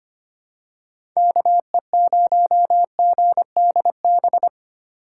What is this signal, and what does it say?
Morse code, heard as a keyed tone, about 25 wpm, 700 Hz. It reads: KE0GDB
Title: USA Call Signs - 6 Characters - 8x Character Spacing - 25wpm